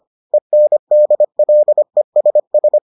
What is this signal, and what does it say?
Morse code, heard as a keyed tone, about 25 wpm, 600 Hz. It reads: ENDLESS